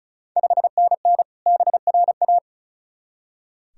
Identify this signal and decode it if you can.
Morse code, heard as a keyed tone, about 36 wpm, 700 Hz. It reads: 5NN BRA
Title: Ontario QSO Party Exchange 35wpm